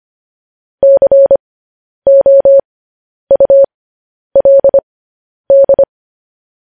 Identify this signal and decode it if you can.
Morse code, heard as a keyed tone, about 25 wpm, spaced as for 10 wpm, 550 Hz. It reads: COULD
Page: news headlines in morse code at 10 wpm